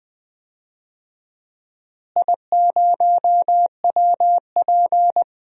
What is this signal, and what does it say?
Morse code, heard as a keyed tone, about 20 wpm, 700 Hz. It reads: I0WP
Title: DXCC-100 Call Signs - 4 Characters - Repeated 3x - 20wpm